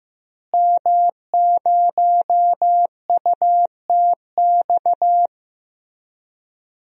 Morse code, heard as a keyed tone, about 16 wpm, 700 Hz. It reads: M0UTX
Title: DXCC-100 Call Signs - 5 Characters - Repeated 2x - 15wpm